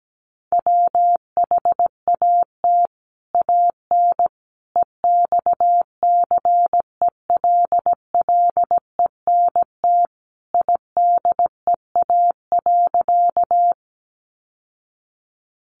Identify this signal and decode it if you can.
Morse code, heard as a keyed tone, about 17 wpm, 700 Hz. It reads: WHAT AN EXCELLENT IDEA.
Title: Sentences from Top 1000 Words 17wpm